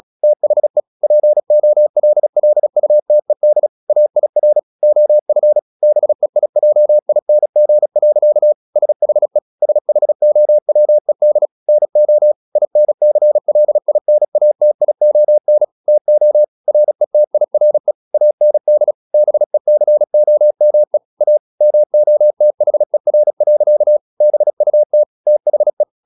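Morse code, heard as a keyed tone, about 36 wpm, 600 Hz. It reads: THE POLLUTED AIR OF BEIJING. SHE SHOWED NO INCLINATION TO RETIRE AND BECOME A MOTHER. BUT THE
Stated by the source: Morse practice at 36 wpm, text of Ken Follett's novel Never, from CW med Gnister.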